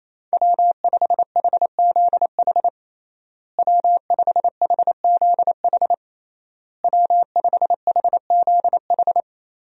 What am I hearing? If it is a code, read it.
Morse code, heard as a keyed tone, about 28 wpm, 700 Hz. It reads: W5HZH W5HZH W5HZH
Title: USA Call Signs - 5 Characters - Repeated 3x - 28wpm